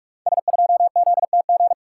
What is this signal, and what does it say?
Morse code, heard as a keyed tone, about 42 wpm, 700 Hz. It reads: S2ZTG
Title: Code Groups - 5 Letter:Numbers - 45wpm